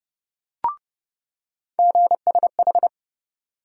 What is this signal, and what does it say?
Morse code, heard as a keyed tone, about 30 wpm, 700 Hz. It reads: GSH
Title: USA Call Sign Postfixes - 3 Letters 30wpm